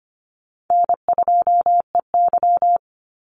Morse code, heard as a keyed tone, about 25 wpm, 700 Hz. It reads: N2EY
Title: Easy Call Signs - 25wpm